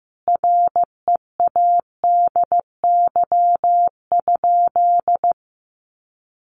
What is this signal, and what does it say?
Morse code, heard as a keyed tone, about 15 wpm, 700 Hz. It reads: READY?
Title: Sentences from Top 700 Words 15wpm